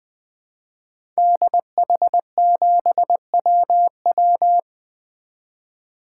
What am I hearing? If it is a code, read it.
Morse code, heard as a keyed tone, about 20 wpm, 700 Hz. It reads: DH7WW
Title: 2023 SOTA Activator Call Signs - 20wpm